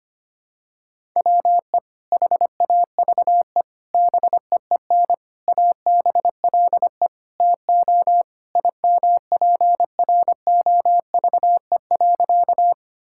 Morse code, heard as a keyed tone, about 25 wpm, 700 Hz. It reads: WE HAVE BEEN ABLE TO IMPROVE.
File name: practice / 25wpm / Sentences with 4-Word Phrases - 25wpm